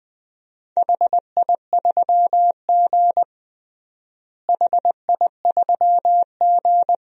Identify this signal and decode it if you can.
Morse code, heard as a keyed tone, about 21 wpm, 700 Hz. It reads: HI3G HI3G
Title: DXCC-100 Call Signs - 4 Characters - Repeated 2x - 20wpm